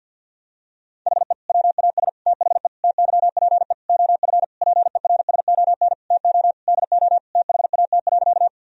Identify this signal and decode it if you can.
Morse code, heard as a keyed tone, about 50 wpm, 700 Hz. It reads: HE WAS THE TYPE OF PERSON TO DO THAT.
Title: Sentences from Top 1000 Words - Review Part 3 - 50wpm